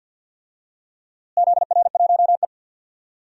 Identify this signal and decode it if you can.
Morse code, heard as a keyed tone, about 52 wpm, 700 Hz. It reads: ZR1E
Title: DXCC-100 Call Signs - 4 Characters - Repeated 2x - 50wpm